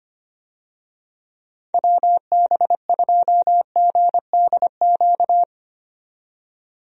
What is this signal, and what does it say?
Morse code, heard as a keyed tone, about 25 wpm, 700 Hz. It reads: WB2GDQ
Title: USA Call Signs - 6 Characters - 25wpm